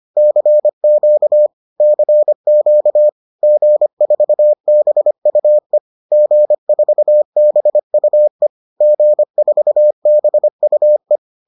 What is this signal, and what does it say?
Morse code, heard as a keyed tone, about 25 wpm, 600 Hz. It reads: CQ CQ G4BUE G4BUE G4BUE